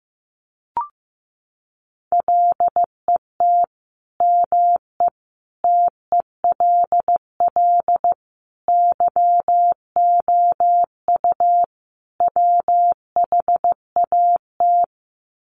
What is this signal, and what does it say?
Morse code, heard as a keyed tone, about 15 wpm, 700 Hz. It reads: LET ME TELL YOU WHAT
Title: Sets of 5 Words Encore 15wpm